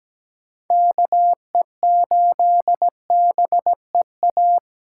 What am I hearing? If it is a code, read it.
Morse code, heard as a keyed tone, about 17 wpm, 700 Hz. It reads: KE8BEA
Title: USA Call Signs - 6 Characters - 8x Character Spacing - 17wpm